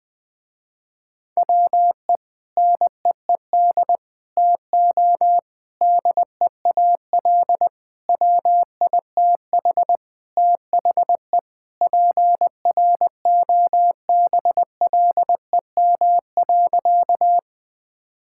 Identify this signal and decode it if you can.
Morse code, heard as a keyed tone, about 20 wpm, 700 Hz. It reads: WE NEED TO DEAL WITH THE PROBLEM.
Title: Sentences with 4-Word Phrases - Spoken Beginning - 20wpm